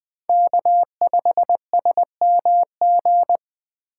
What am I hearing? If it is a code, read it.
Morse code, heard as a keyed tone, about 20 wpm, 700 Hz. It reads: K5SMG
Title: USA Call Signs - 5 Characters - 3x Character Spacing - 20wpm